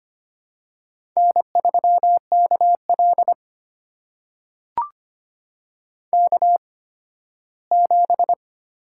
Morse code, heard as a keyed tone, about 25 wpm, spaced as for 15 wpm, 700 Hz. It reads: N3KL K7
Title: USA Call Signs - 4 Characters - 8x Character Spacing - 25wpm